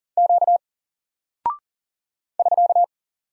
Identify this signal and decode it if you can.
Morse code, heard as a keyed tone, about 40 wpm, 700 Hz. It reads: Q <SK>
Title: Single Character (Letter:Number:Punctuation:Prosign) 40wpm